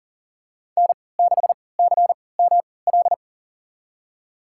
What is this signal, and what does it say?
Morse code, heard as a keyed tone, about 40 wpm, spaced as for 23 wpm, 700 Hz. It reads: N6CML
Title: USA Call Signs - 5 Characters - 3x Character Spacing - 40wpm